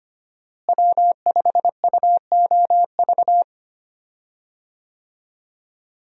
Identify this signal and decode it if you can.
Morse code, heard as a keyed tone, about 25 wpm, 690 Hz. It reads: W5UOV
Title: USA Call Signs - 5 Characters - Repeated 2x - 25wpm